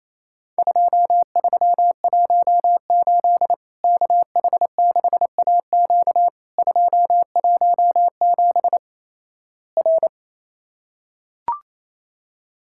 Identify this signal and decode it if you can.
Morse code, heard as a keyed tone, about 28 wpm, 700 Hz. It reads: 2318 KH6AQ 217 R
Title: Worked All Europe DX Contest - QTC Exchange - 28wpm